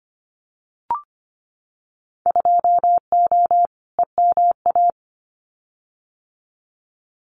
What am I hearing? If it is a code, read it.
Morse code, heard as a keyed tone, about 25 wpm, 700 Hz. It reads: 2O EMA
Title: Winter Field Day Exchanges 25wpm